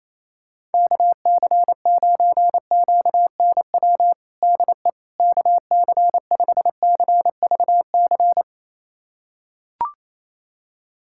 Morse code, heard as a keyed tone, about 28 wpm, 700 Hz. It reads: KC9QNW DE KC5CVC
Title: USA Callsign DE Callsign 28wpm